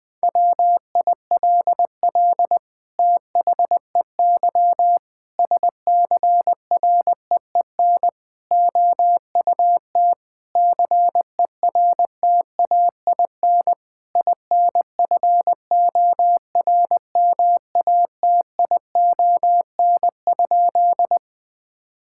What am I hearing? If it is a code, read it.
Morse code, heard as a keyed tone, about 20 wpm, 700 Hz. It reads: WILL THEY SCREEN OUT CERTAIN INFORMATION?
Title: Sentences from Top 1000 Words - Review Part 6 - 20wpm